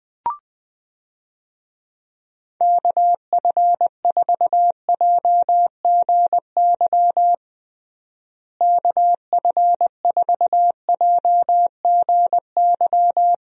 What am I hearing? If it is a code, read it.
Morse code, heard as a keyed tone, about 20 wpm, 700 Hz. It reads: KF4JGY KF4JGY
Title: USA Call Signs - 6 Characters - Repeated 2x - 20wpm